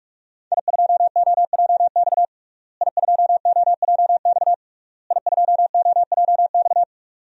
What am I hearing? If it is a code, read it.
Morse code, heard as a keyed tone, about 41 wpm, 700 Hz. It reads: I2OJX I2OJX I2OJX
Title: DXCC-100 Call Signs - 5 Characters - Repeated 3x - 45wpm